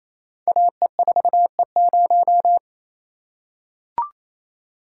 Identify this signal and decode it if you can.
Morse code, heard as a keyed tone, about 29 wpm, 700 Hz. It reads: AE4E0
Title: Code Groups - 5 Letter:Numbers - 28wpm